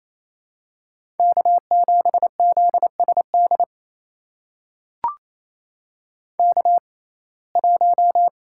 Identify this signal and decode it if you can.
Morse code, heard as a keyed tone, about 28 wpm, spaced as for 20 wpm, 700 Hz. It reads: K7ZSD K1
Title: USA Call Signs - 5 Characters - 6x Character Spacing - 28wpm